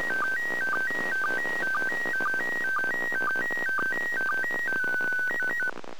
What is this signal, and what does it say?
SSTV (slow-scan television), PD 120.
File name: 2020 0930 ISS SSTV